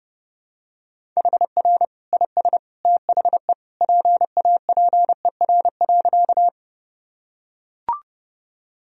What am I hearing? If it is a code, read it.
Morse code, heard as a keyed tone, about 30 wpm, 700 Hz. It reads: HR IS THE PAPER.
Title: Sentences from Top 300 Words and Abbreviations 30wpm